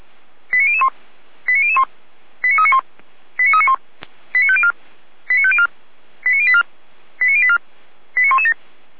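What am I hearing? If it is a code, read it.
Five-tone selective call (ZVEI), heard as quick runs of five tones, tones of about 70 ms. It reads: 89001 89001 89391 89391 89504 89504 89084 89084 89117
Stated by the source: ZVEI five-tone call received by SDR, fire brigade Feuerwehr Orlamünde.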